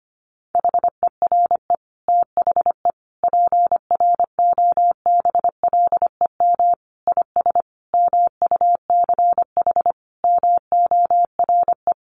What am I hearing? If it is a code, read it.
Morse code, heard as a keyed tone, about 25 wpm, 700 Hz. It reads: HERE THE PROBLEM IS MUCH MORE
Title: Sentences from Top 1200 Words 25wpm